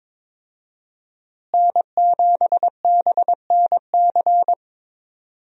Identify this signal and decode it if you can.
Morse code, heard as a keyed tone, about 22 wpm, 700 Hz. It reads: N7BNC
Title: USA Call Signs - 5 Characters - Repeated 2x - 22wpm